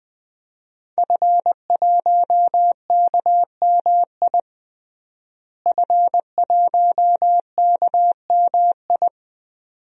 Morse code, heard as a keyed tone, about 20 wpm, 700 Hz. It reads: F1KMI F1KMI
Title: DXCC-100 Call Signs - 5 Characters - Repeated 2x - 20wpm